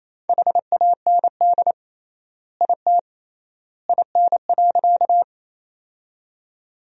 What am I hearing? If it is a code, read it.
Morse code, heard as a keyed tone, about 28 wpm, 700 Hz. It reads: HAND IT IN.
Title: 3x Word-Spacing - Sentences from Top 200 Words - 28wpm